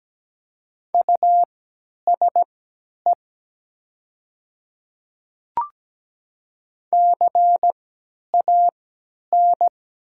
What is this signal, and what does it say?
Morse code, heard as a keyed tone, about 17 wpm, spaced as for 10 wpm, 700 Hz. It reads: USE CAN
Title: 3x Character Spacing - Top 100 Words - 17wpm